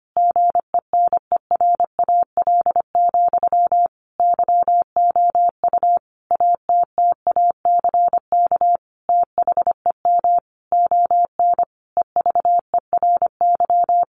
Morse code, heard as a keyed tone, about 25 wpm, 700 Hz. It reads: GENERAL, YOU ATTACK THEM ON EVERY
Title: Sentences from Top 1500 Words 25wpm